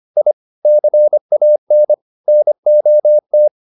Morse code, heard as a keyed tone, about 25 wpm, 600 Hz. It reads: I CAN NOT